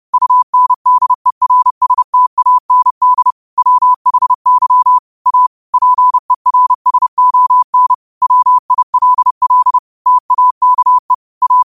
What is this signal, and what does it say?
Morse code, heard as a keyed tone, about 30 wpm, 1000 Hz. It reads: UNDERSTAND WHY A PERSON WILL TAKE A
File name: qotdfiles / 30wpm